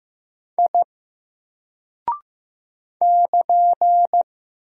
Morse code, heard as a keyed tone, about 15 wpm, 700 Hz. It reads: I <KN>
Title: Single Character (Letter:Number:Punctuation:Prosign) 15wpm